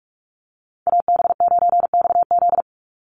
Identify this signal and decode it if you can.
Morse code, heard as a keyed tone, about 44 wpm, 700 Hz. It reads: AB9XZ
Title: USA Call Signs 45wpm